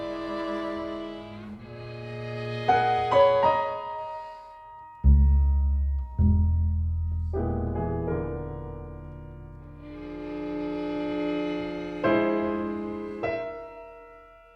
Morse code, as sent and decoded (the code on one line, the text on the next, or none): none